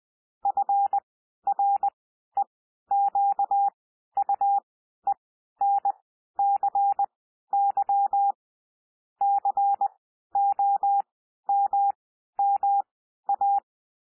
Morse code, heard as a keyed tone, about 20 wpm, spaced as for 12 wpm, 800 Hz. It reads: FREQUENCY COMMA
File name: cwmp3 / P1 SOUND PASSAGE 20 wpm